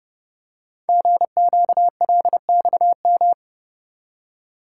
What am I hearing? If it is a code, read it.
Morse code, heard as a keyed tone, about 30 wpm, 700 Hz. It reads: GQLXM
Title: Code Groups - 5 Letters - 30wpm